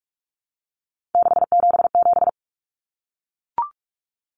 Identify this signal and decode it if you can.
Morse code, heard as a keyed tone, about 44 wpm, 700 Hz. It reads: 677
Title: Common Cut Numbers - 3-Digit 45wpm